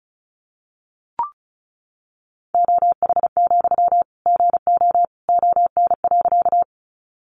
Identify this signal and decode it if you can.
Morse code, heard as a keyed tone, about 34 wpm, 700 Hz. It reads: OH, GO ON.
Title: Sentences from Top 500 Words 35wpm